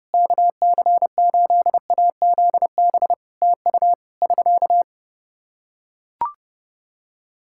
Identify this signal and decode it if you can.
Morse code, heard as a keyed tone, about 30 wpm, 700 Hz. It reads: KC8AZB TU <SK>
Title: Practice QSO - Exchange 4 - Ending - 30wpm